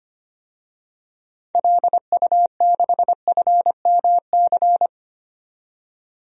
Morse code, heard as a keyed tone, about 25 wpm, 700 Hz. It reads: LU6FMC
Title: DXCC-100 Call Signs - 6 Characters - 25wpm